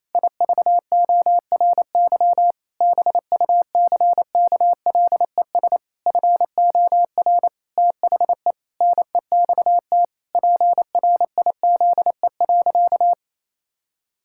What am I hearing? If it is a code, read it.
Morse code, heard as a keyed tone, about 28 wpm, 700 Hz. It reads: IVORY BUCKLES FOR THE NEXT PRIZE.
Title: Pangram Challenge v2 - 28wpm